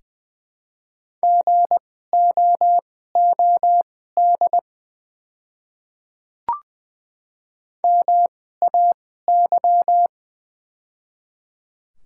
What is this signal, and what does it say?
Morse code, heard as a keyed tone, about 20 wpm, spaced as for 14 wpm, 700 Hz. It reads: GOOD MAY
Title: 2x Character Spacing - Top 200 Words - 20wpm